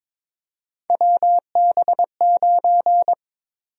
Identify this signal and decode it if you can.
Morse code, heard as a keyed tone, about 23 wpm, 700 Hz. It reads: WB9
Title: USA Call Sign Prefixes - Repeated 2x - 22wpm